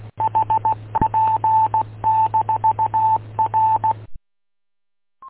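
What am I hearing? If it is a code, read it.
Morse code, heard as a keyed tone, about 16 wpm, 880 Hz. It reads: HP-R
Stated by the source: Morse heard on W8UM 145.230 repeater audio archive.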